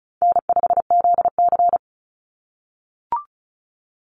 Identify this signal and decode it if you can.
Morse code, heard as a keyed tone, about 34 wpm, 700 Hz. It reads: N5ZC
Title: NA SOTA Chaser Callsigns 35wpm